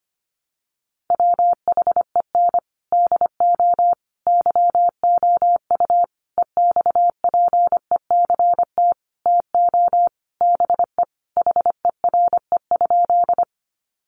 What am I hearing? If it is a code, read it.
Morse code, heard as a keyed tone, about 25 wpm, 700 Hz. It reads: WHEN DO YOU EXPECT TO BE HERE?
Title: Sentences from Top 1100 Words 25wpm